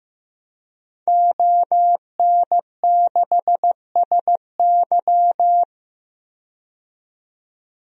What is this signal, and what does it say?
Morse code, heard as a keyed tone, about 15 wpm, 700 Hz. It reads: ON6SY
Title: DXCC-100 Call Signs - 5 Characters - 15wpm